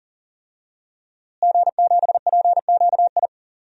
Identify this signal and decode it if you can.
Morse code, heard as a keyed tone, about 40 wpm, 700 Hz. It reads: G7PQI
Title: DXCC-100 Call Signs - 5 Characters - Repeated 2x - 40wpm